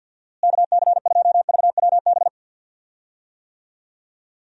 Morse code, heard as a keyed tone, about 46 wpm, 700 Hz. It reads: KC2VWB
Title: USA Call Signs - 6 Characters - Repeated 3x - 50wpm